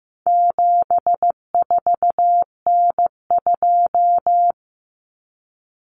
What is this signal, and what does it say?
Morse code, heard as a keyed tone, about 15 wpm, 700 Hz. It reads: 74N2
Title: Common Cut Numbers - 4-Digit 15wpm